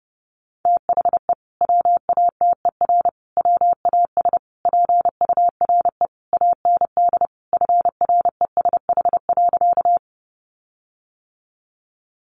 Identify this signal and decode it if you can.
Morse code, heard as a keyed tone, about 30 wpm, 700 Hz. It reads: THE WATER WAS PURE AND FRESH.
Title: Sentences from Top 1500 Words 30wpm